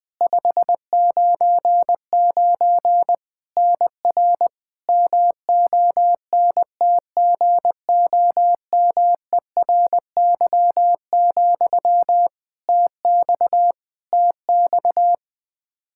Morse code, heard as a keyed tone, about 20 wpm, 700 Hz. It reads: 599 NR MONTGOMERY, TX TX
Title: Practice QSO - Exchange 1 - Beginning - 20wpm